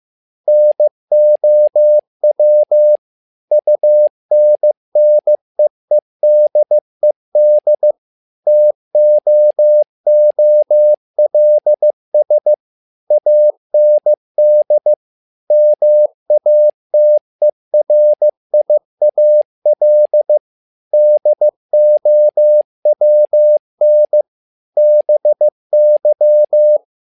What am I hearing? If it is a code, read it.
Morse code, heard as a keyed tone, about 15 wpm, 600 Hz. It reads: NOW UNNEEDED TOOLS AND MATERIAL DOWN BY
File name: learning center / tower trouble / tower trouble 15wpm